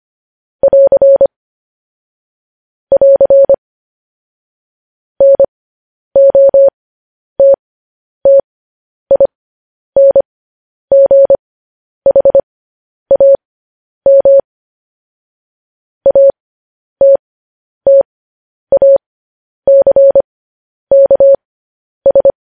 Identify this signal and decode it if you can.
Morse code, heard as a keyed tone, about 25 wpm, spaced as for 10 wpm, 550 Hz. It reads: <AR> <AR> NOTTINGHAM ATTACKS
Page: news-headlines-in-morse-code-at-10-wpm